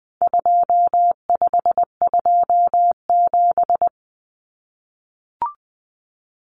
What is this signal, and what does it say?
Morse code, heard as a keyed tone, about 20 wpm, 700 Hz. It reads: 2527
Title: Numbers - 4-Digit 20wpm